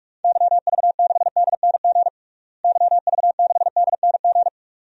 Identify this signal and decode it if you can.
Morse code, heard as a keyed tone, about 46 wpm, 700 Hz. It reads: YV6DNG YV6DNG
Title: DXCC-100 Call Signs - 6 Characters - Repeated 2x - 45wpm